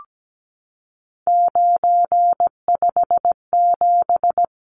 Morse code, heard as a keyed tone, about 17 wpm, 700 Hz. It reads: 957
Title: Numbers - 3-Digit 17wpm